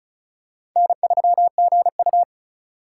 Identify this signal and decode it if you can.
Morse code, heard as a keyed tone, about 36 wpm, 700 Hz. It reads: N3GU
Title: USA Call Signs - 4 Characters - 4x Character Spacing - 35wpm